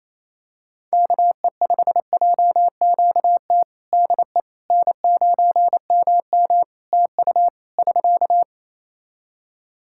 Morse code, heard as a keyed tone, about 28 wpm, 700 Hz. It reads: KE5JQT DE N9MM TU <SK>
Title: K1USN SST QSO Practice 28wpm